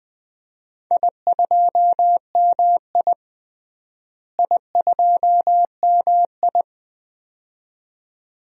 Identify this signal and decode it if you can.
Morse code, heard as a keyed tone, about 21 wpm, 700 Hz. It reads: I2MI I2MI
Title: DXCC-100 Call Signs - 4 Characters - Repeated 2x - 20wpm